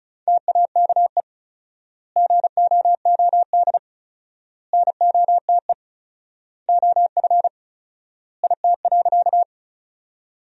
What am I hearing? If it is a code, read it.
Morse code, heard as a keyed tone, about 34 wpm, 700 Hz. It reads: TAKE GOOD NOTE OF IT.
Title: 4x Word-Spacing - Sentences from Top 500 Words - 35wpm